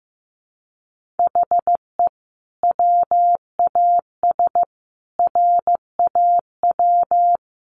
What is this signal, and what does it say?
Morse code, heard as a keyed tone, about 15 wpm, 700 Hz. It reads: HE WAS RAW
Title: Sentences from Top 1500 Words 15wpm